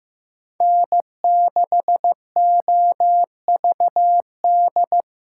Morse code, heard as a keyed tone, about 18 wpm, 700 Hz. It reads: N6OVD
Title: Speed Racing - USA Call Signs 15wpm